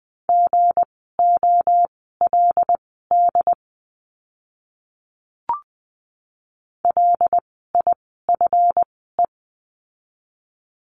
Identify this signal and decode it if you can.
Morse code, heard as a keyed tone, about 20 wpm, spaced as for 15 wpm, 700 Hz. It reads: GOLD LIFE
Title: 160 Lesson 26f - Letter g - Words 20wpm